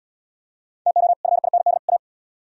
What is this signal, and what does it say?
Morse code, heard as a keyed tone, about 67 wpm, 700 Hz. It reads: TO FIND A